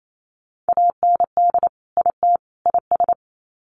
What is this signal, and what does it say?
Morse code, heard as a keyed tone, about 28 wpm, 700 Hz. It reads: AND IT IS
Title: Sets of 3 Words 28wpm